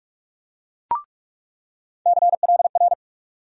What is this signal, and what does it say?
Morse code, heard as a keyed tone, about 46 wpm, 700 Hz. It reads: CLR
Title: USA Call Sign Postfixes - 3 Letters 45wpm